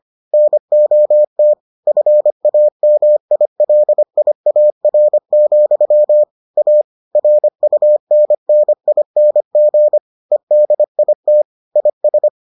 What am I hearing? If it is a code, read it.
Morse code, heard as a keyed tone, about 25 wpm, 600 Hz. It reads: NOT FAMILIAR, A RUNNING EDIT IS